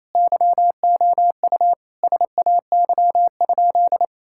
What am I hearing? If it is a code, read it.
Morse code, heard as a keyed tone, about 28 wpm, 700 Hz. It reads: YOU SAY?
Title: Sentences from Top 1600 Words 28wpm